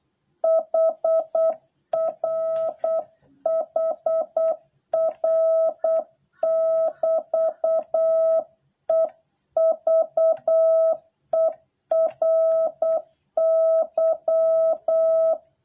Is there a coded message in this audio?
Yes: Morse code from 0.4 to 15.3 s, heard as a keyed tone, about 8 wpm, 650 Hz. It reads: HR HR <BT> EVERY